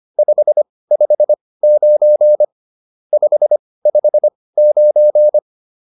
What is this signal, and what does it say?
Morse code, heard as a keyed tone, about 25 wpm, spaced as for 18 wpm, 600 Hz. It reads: 559 559
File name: QSO405 18